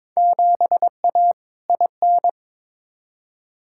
Morse code, heard as a keyed tone, about 22 wpm, 700 Hz. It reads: 7A IN
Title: Field Day Exchanges Encore - 22wpm